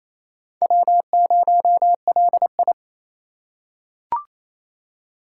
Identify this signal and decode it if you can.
Morse code, heard as a keyed tone, about 28 wpm, 700 Hz. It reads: W0LI
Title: USA Call Signs - 4x Character Spacing - 28wpm